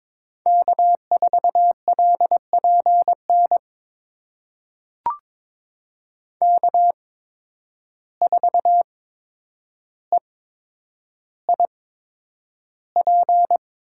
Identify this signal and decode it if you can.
Morse code, heard as a keyed tone, about 22 wpm, spaced as for 9 wpm, 700 Hz. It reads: K4LPN K4EIP
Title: USA Call Signs - 5 Characters - 8x Character Spacing - 22wpm